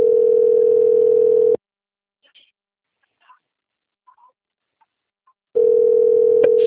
Ringback tone (North America), 440 + 480 Hz.